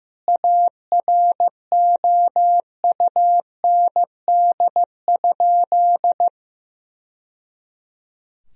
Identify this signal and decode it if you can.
Morse code, heard as a keyed tone, about 15 wpm, 700 Hz. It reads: AROUND?
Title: Sentences from Top 1000 Words - Review Part 4 - 15wpm